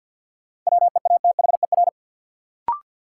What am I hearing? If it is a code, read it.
Morse code, heard as a keyed tone, about 53 wpm, 700 Hz. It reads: WEATHER
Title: Sentences from Top 800 Words - New Word Drill - 50wpm